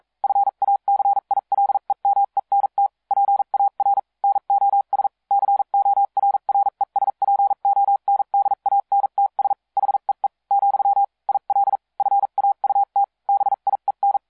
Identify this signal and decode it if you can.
Morse code, heard as a keyed tone, about 42 wpm, spaced as for 36 wpm, 800 Hz. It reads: FACILEMENT PAR NOS CORRESPONDANTS HEE , IL FAUT BIEN